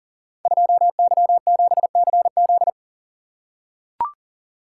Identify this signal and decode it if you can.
Morse code, heard as a keyed tone, about 42 wpm, 690 Hz. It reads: 2Y7CZ
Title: Code Groups - 5 Letter:Numbers - 40wpm